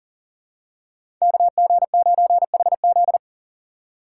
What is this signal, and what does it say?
Morse code, heard as a keyed tone, about 40 wpm, 700 Hz. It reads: KG9HZ
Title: USA Call Signs - 6x Character Spacing - 40wpm